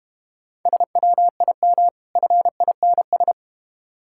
Morse code, heard as a keyed tone, about 31 wpm, 690 Hz. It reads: SWIM FINS
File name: Easy Words - Speed-Racing with Hints - 25wpm